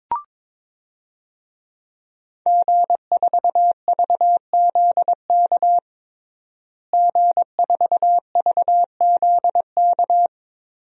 Morse code, heard as a keyed tone, about 21 wpm, 700 Hz. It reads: G4VZK G4VZK
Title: DXCC-100 Call Signs - 5 Characters - Repeated 2x - 22wpm